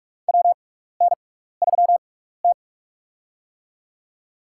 Morse code, heard as a keyed tone, about 46 wpm, spaced as for 16 wpm, 700 Hz. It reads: WN3T
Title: USA Call Signs - 4 Characters - 6x Character Spacing - 45wpm